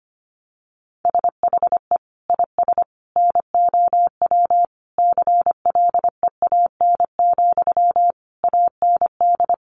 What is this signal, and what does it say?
Morse code, heard as a keyed tone, about 25 wpm, 700 Hz. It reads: SHE IS NOW CLEAN, AND
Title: Sentences from Top 1100 Words 25wpm